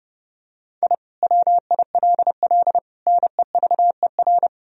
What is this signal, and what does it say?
Morse code, heard as a keyed tone, about 31 wpm, 700 Hz. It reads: I WILL NEVER
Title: Sentences from Top 1000 Words - Review Part 1 - 30wpm